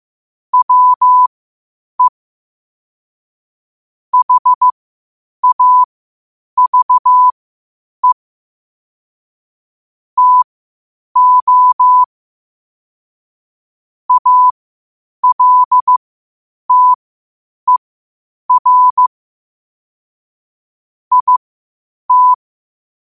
Morse code, heard as a keyed tone, about 15 wpm, spaced as for 8 wpm, 1000 Hz. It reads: WE HAVE TO ALTER IT
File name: qotdfiles / 8wpm